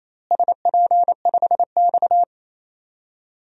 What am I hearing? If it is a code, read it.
Morse code, heard as a keyed tone, about 28 wpm, 700 Hz. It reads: SP5X
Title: DXCC-100 Call Signs - 4 Characters - Repeated 2x - 28wpm